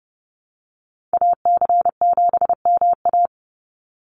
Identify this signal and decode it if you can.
Morse code, heard as a keyed tone, about 30 wpm, 700 Hz. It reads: AC7MA
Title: NA SOTA Activator Callsigns 30wpm